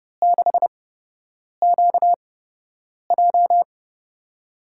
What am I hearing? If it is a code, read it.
Morse code, heard as a keyed tone, about 32 wpm, spaced as for 8 wpm, 700 Hz. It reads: 6QJ
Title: USA Call Signs - 8x Character Spacing - 30wpm